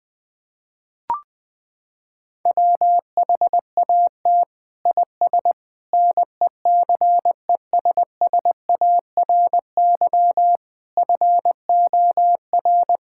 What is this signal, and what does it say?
Morse code, heard as a keyed tone, about 20 wpm, 700 Hz. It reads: WHAT IS NECESSARY FOR
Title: Sentences from Top 1000 Words - Review Part 5 - 20wpm